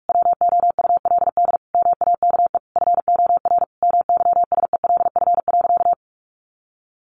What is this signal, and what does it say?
Morse code, heard as a keyed tone, about 44 wpm, 700 Hz. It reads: WOULD MAKE FOR MYSELF.
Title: Sentences from Top 1300 Words 45wpm